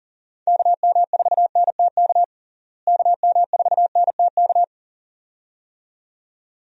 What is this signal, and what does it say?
Morse code, heard as a keyed tone, about 37 wpm, 700 Hz. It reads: KM4NTK KM4NTK
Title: USA Call Signs - 6 Characters - Repeated 2x - 40wpm